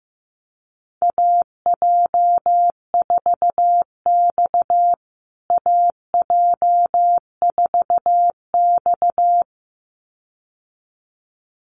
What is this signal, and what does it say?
Morse code, heard as a keyed tone, about 15 wpm, 700 Hz. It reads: AJ4X AJ4X
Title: USA Call Signs Repeated 2x 15wpm